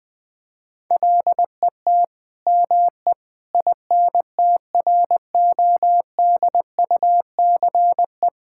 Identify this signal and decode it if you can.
Morse code, heard as a keyed tone, about 20 wpm, 700 Hz. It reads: LET ME INTRODUCE
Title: Sentences from Top 1600 Words 20wpm